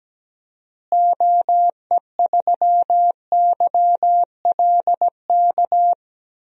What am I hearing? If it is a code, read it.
Morse code, heard as a keyed tone, about 17 wpm, 700 Hz. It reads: OE3YLK
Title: DXCC-100 Call Signs - 6 Characters - Repeated 3x - 17wpm